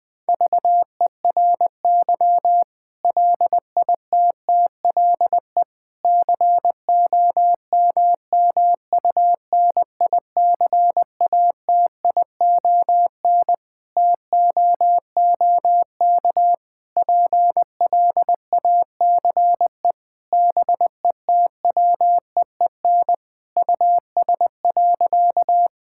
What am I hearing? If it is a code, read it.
Morse code, heard as a keyed tone, about 21 wpm, 700 Hz. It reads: VERY LITTLE COMMUNICATION TOOK PLACE BETWEEN US.
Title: Sentences from Top 1000 Words - Review Part 6 - 20wpm